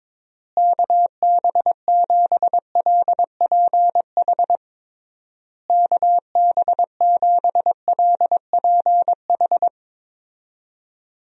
Morse code, heard as a keyed tone, about 22 wpm, 700 Hz. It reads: KB7LPH KB7LPH
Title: USA Call Signs - 6 Characters - Repeated 3x - 22wpm